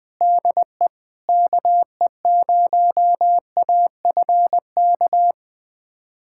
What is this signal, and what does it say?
Morse code, heard as a keyed tone, about 21 wpm, 700 Hz. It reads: DE KE0AFK
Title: USA Callsign DE Callsign 20wpm